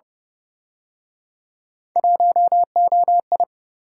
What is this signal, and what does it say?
Morse code, heard as a keyed tone, about 32 wpm, 700 Hz. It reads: 1OI